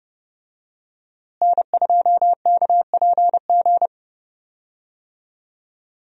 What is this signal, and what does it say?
Morse code, heard as a keyed tone, about 30 wpm, 700 Hz. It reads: N2KPG
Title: USA Call Signs - 5 Characters - Repeated 2x - 30wpm